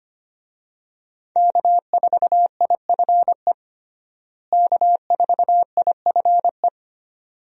Morse code, heard as a keyed tone, about 25 wpm, 700 Hz. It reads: K4IFE K4IFE
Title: USA Call Signs - 5 Characters - Repeated 2x - 25wpm